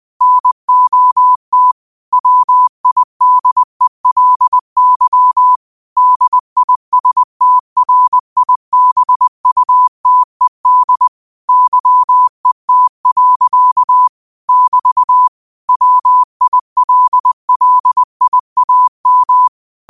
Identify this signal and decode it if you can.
Morse code, heard as a keyed tone, about 20 wpm, 1000 Hz. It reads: NOT WIDELY DISTRIBUTED YET. <BT> WILLIAM
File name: qotdfiles / 20wpm